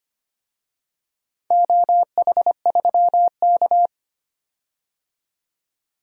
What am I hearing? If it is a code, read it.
Morse code, heard as a keyed tone, about 25 wpm, 700 Hz. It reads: OH3K
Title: DXCC-100 Call Signs - 4 Characters - 25wpm